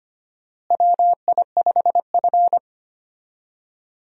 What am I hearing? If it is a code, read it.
Morse code, heard as a keyed tone, about 25 wpm, 700 Hz. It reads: WI5F
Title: USA Call Signs - 4 Characters - Repeated 3x - 25wpm